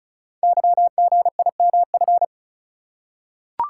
Morse code, heard as a keyed tone, about 36 wpm, 700 Hz. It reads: YGIMF
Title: Code Groups - 5 Letter:Numbers - 35wpm